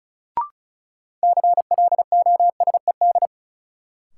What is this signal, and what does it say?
Morse code, heard as a keyed tone, about 36 wpm, 700 Hz. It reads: CLOSED
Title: Sentences from Top 1400 Words - New Word Drill Alt - 35wpm